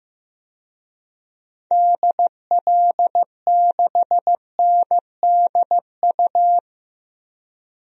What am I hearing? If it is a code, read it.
Morse code, heard as a keyed tone, about 16 wpm, 700 Hz. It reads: DL6NDU
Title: DXCC-100 Call Signs - 6 Characters - Repeated 2x - 15wpm